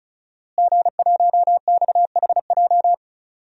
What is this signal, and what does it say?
Morse code, heard as a keyed tone, about 36 wpm, 690 Hz. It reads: G1XHJ
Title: DXCC-100 Call Signs - 5 Characters - Repeated 3x - 35wpm